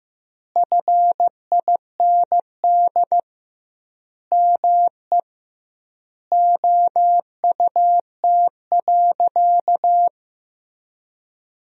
Morse code, heard as a keyed tone, about 15 wpm, 700 Hz. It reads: FIND ME OUT.
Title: 2x Word-Spacing - Sentences from Top 500 Words - 15wpm